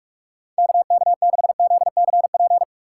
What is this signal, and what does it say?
Morse code, heard as a keyed tone, about 43 wpm, 700 Hz. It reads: KK6ZCP
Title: USA Call Signs - 6 Characters - 6x Character Spacing - 45wpm